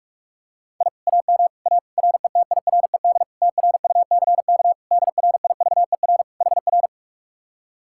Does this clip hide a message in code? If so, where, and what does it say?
Yes: Morse code from 0.8 to 6.9 s, heard as a keyed tone, about 49 wpm, 700 Hz. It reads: I AM A RETIRED TRUCK DRIVER HR